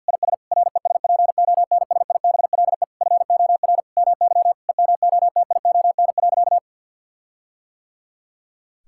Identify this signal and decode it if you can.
Morse code, heard as a keyed tone, about 51 wpm, 700 Hz. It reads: IS RESPONSIBLE FOR MY EMOTION.
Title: Sentences from Top 1600 Words 50wpm